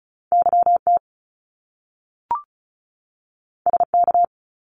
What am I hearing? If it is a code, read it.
Morse code, heard as a keyed tone, about 36 wpm, 700 Hz. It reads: YT SK
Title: Canadian Provinces and Territories 35wpm